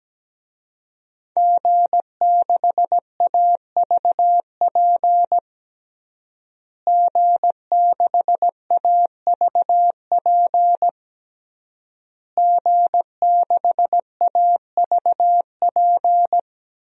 Morse code, heard as a keyed tone, about 17 wpm, 700 Hz. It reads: G6AVP G6AVP G6AVP
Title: DXCC-100 Call Signs - 5 Characters - Repeated 3x - 17wpm